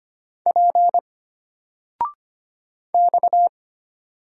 Morse code, heard as a keyed tone, about 25 wpm, 700 Hz. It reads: P X
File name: Single-Letter-Number-25wpm